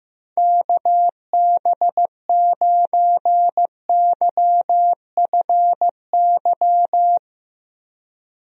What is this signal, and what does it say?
Morse code, heard as a keyed tone, about 15 wpm, 700 Hz. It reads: KB9YFY
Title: USA Call Signs - 6 Characters - Repeated 3x - 15wpm